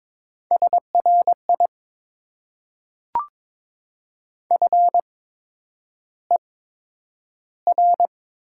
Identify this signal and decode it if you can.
Morse code, heard as a keyed tone, about 22 wpm, spaced as for 9 wpm, 700 Hz. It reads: SRI FER
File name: Top 90 QSO Elements - 8x Character Spacing - 22wpm